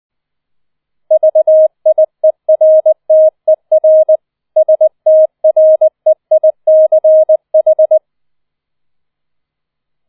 Morse code, heard as a keyed tone, about 20 wpm, 630 Hz. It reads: VIERTER STREICH
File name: 004 Vierter Streich 20wpm